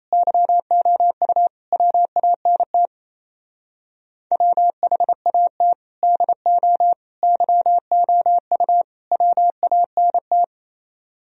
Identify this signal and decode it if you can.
Morse code, heard as a keyed tone, about 35 wpm, 700 Hz. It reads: YOU WANT WHAT DO YOU WANT
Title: Speed-Racing - Sets of 4 Words 28wpm